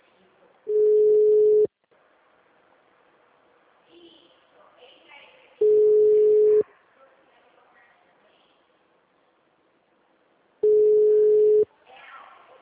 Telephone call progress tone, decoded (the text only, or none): ringback tone